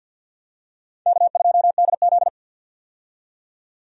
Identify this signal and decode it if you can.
Morse code, heard as a keyed tone, about 53 wpm, 700 Hz. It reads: K2DZ
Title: USA Call Signs - 4 Characters - 50wpm